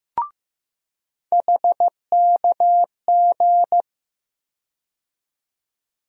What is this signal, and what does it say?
Morse code, heard as a keyed tone, about 16 wpm, 700 Hz. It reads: HKG
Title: USA Call Sign Postfixes - 3 Letters 15wpm